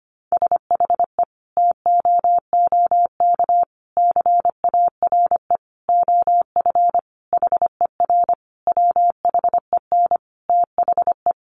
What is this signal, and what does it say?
Morse code, heard as a keyed tone, about 25 wpm, 700 Hz. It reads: SHE TOOK CARE OF HER WHEN THE
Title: Sentences from Top 1300 Words 25wpm